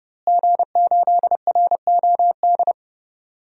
Morse code, heard as a keyed tone, about 30 wpm, 700 Hz. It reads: G8ROD
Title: DXCC-100 Call Signs - 5 Characters - Repeated 3x - 30wpm